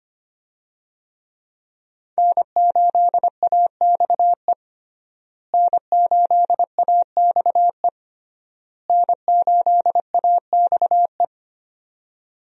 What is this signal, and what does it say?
Morse code, heard as a keyed tone, about 25 wpm, 700 Hz. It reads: N8AXE N8AXE N8AXE
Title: USA Call Signs - 5 Characters - Repeated 3x - 25wpm